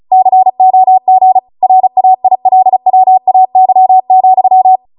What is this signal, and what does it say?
Morse code, heard as a keyed tone, about 36 wpm, 750 Hz. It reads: COG RAILWAY,
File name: Morse / 35WPM